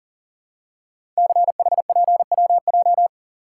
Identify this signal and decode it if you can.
Morse code, heard as a keyed tone, about 42 wpm, 700 Hz. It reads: CHPWJ
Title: Code Groups - 5 Letters - 40wpm